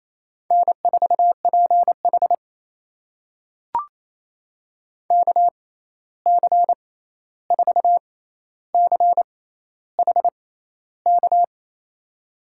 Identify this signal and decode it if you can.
Morse code, heard as a keyed tone, about 28 wpm, spaced as for 13 wpm, 700 Hz. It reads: N4PH KC4CHK
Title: USA Call Signs - 6x Character Spacing - 28wpm